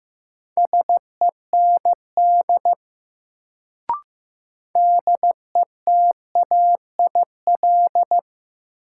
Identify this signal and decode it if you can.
Morse code, heard as a keyed tone, about 15 wpm, 700 Hz. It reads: SEND DETAIL
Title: Sentences from Top 1500 Words - New Word Drill - 15wpm